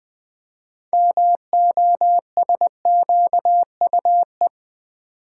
Morse code, heard as a keyed tone, about 20 wpm, 700 Hz. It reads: MOSQUE
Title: Build A Word - 2 Letters At A Time - 6-Letter Words - 20wpm